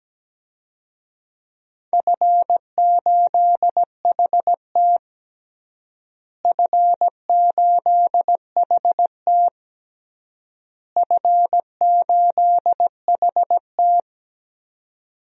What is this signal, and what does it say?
Morse code, heard as a keyed tone, about 17 wpm, 700 Hz. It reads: F8HT F8HT F8HT
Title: DXCC-100 Call Signs - 4 Characters - Repeated 3x - 17wpm